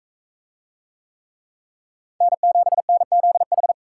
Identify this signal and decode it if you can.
Morse code, heard as a keyed tone, about 42 wpm, 700 Hz. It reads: N7NZH